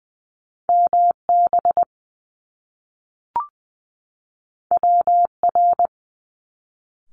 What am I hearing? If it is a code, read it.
Morse code, heard as a keyed tone, about 20 wpm, 700 Hz. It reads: M B W R
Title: Top Two Letter Combinations in Words 20wpm